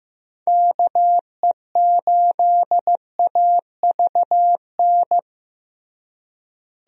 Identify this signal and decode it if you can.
Morse code, heard as a keyed tone, about 15 wpm, 700 Hz. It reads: KE8AVN
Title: USA Call Signs - 6 Characters - Repeated 3x - 15wpm